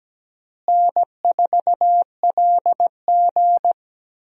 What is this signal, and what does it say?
Morse code, heard as a keyed tone, about 17 wpm, 700 Hz. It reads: N4LG
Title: USA Call Signs - 4 Characters - Repeated 3x - 17wpm